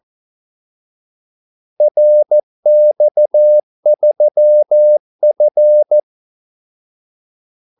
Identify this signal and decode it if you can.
Morse code, heard as a keyed tone, about 14 wpm, 600 Hz. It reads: RX3F
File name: Callsigns-14wpm